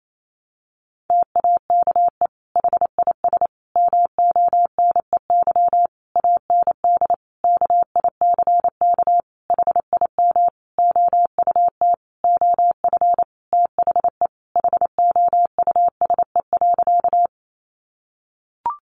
Morse code, heard as a keyed tone, about 28 wpm, 700 Hz. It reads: TAKE HIS MONEY AND KICK HIM OUT OF THE HOUSE.
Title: Sentences from Top 1300 Words 28wpm